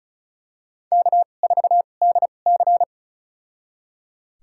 Morse code, heard as a keyed tone, about 34 wpm, spaced as for 25 wpm, 700 Hz. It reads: K4DC
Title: USA Call Signs - 4 Characters - 2x Character Spacing - 35wpm